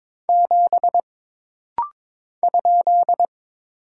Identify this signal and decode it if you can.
Morse code, heard as a keyed tone, about 23 wpm, 700 Hz. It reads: 7 ?
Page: This rapid-fire timing is geared to push your ICR (Instant Character Recognition) skills.